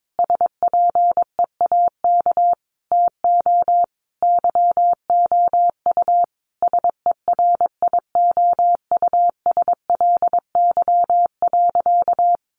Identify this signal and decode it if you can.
Morse code, heard as a keyed tone, about 22 wpm, 700 Hz. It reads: SPEAK TO YOU SERIOUSLY.
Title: Sentences from Top 1200 Words 22wpm